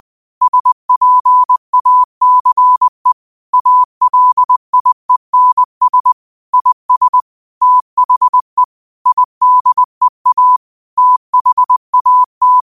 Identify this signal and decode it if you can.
Morse code, heard as a keyed tone, about 20 wpm, 1000 Hz. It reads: SPACE ALIENS IS THE IDEA THAT
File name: qotdfiles / 20wpm